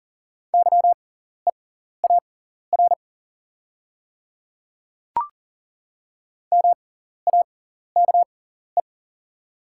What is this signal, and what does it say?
Morse code, heard as a keyed tone, about 39 wpm, spaced as for 14 wpm, 700 Hz. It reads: YEAR MAKE
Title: 6x Character Spacing - Top 200 Words - 40wpm